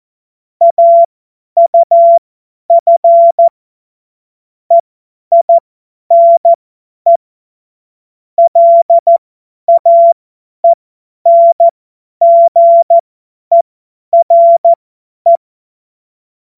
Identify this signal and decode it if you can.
Morse code, heard as a keyed tone, about 14 wpm, spaced as for 10 wpm, 680 Hz. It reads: AUF EINE LAENGERE
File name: Morsekurs / Witze / Witz 20 Februar 2024 50 F70